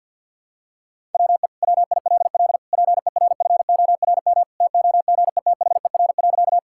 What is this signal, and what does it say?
Morse code, heard as a keyed tone, about 48 wpm, 700 Hz. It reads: WE WILL PERFORM TOGETHER.
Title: Sentences from Top 1000 Words - Review Part 5 - 50wpm